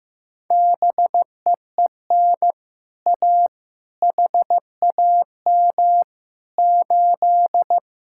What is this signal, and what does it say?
Morse code, heard as a keyed tone, about 15 wpm, 700 Hz. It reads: BEEN A HAM 8
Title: Speed-Racing Common QSO Phrases Encore - 15wpm